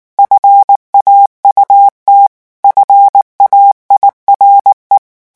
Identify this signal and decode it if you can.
Morse code, heard as a keyed tone, about 19 wpm, 800 Hz. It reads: FAUT FAIRE
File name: morse mp3 / revision9 citations humoristiques vitesse 18 mots minute